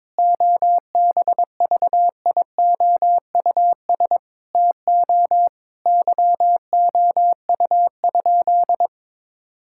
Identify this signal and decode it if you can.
Morse code, heard as a keyed tone, about 21 wpm, 700 Hz. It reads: OBVIOUS TO YOU?
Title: Sentences from Top 1600 Words 22wpm